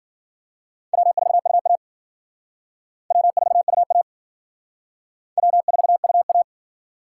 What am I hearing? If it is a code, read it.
Morse code, heard as a keyed tone, about 52 wpm, 700 Hz. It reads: W4UA W4UA W4UA